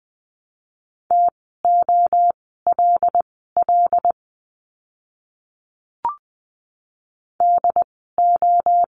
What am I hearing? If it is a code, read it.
Morse code, heard as a keyed tone, about 20 wpm, spaced as for 14 wpm, 700 Hz. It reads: TOLL DO
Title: 075 Lesson 13f - Letter L - Words 20wpm